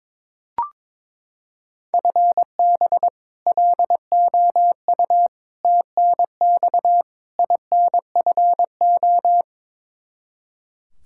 Morse code, heard as a keyed tone, about 22 wpm, 700 Hz. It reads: FB LOU TNX INFO
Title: K1USN SST QSO Practice 22wpm